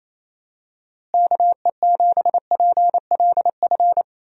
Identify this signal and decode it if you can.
Morse code, heard as a keyed tone, about 28 wpm, 700 Hz. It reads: KE7PLF
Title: USA Call Signs - 6 Characters - Repeated 3x - 28wpm